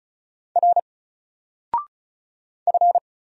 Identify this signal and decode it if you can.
Morse code, heard as a keyed tone, about 35 wpm, 700 Hz. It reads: R F
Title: Single Character (Letter:Number:Punctuation:Prosign) 35wpm